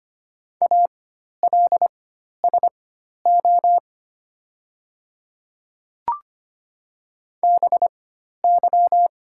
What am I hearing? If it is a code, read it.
Morse code, heard as a keyed tone, about 25 wpm, spaced as for 12 wpm, 700 Hz. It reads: ALSO BY
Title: 4x Character Spacing - Top 300 Words - 25wpm